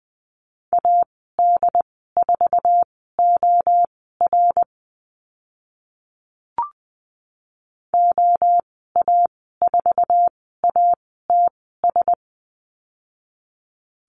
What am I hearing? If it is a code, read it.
Morse code, heard as a keyed tone, about 20 wpm, spaced as for 14 wpm, 700 Hz. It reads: AD4OR OA4ATS
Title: 076 Lesson 13g - Letter L - Call Signs 20wpm